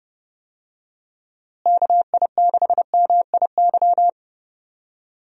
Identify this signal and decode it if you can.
Morse code, heard as a keyed tone, about 32 wpm, 700 Hz. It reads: KI6MIY
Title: USA Call Signs - 6 Characters - Repeated 2x - 30wpm